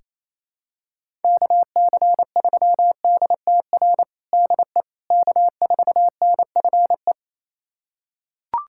Morse code, heard as a keyed tone, about 28 wpm, 700 Hz. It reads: KC3DTR DE K4NFE
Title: USA Callsign DE Callsign 28wpm